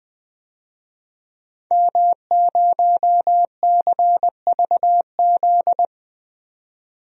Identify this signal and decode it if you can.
Morse code, heard as a keyed tone, about 20 wpm, 700 Hz. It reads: M0CVZ
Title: DXCC-100 Call Signs - 5 Characters - 20wpm